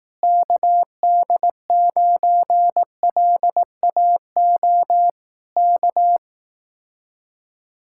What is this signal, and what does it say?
Morse code, heard as a keyed tone, about 18 wpm, 700 Hz. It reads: KD9LAO K
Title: Speed-Racing Common QSO Phrases Encore - 15wpm